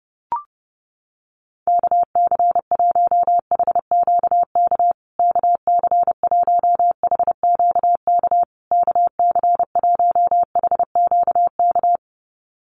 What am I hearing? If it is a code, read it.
Morse code, heard as a keyed tone, about 30 wpm, 700 Hz. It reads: KC1HQK KC1HQK KC1HQK
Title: USA Call Signs Repeated 3x 30wpm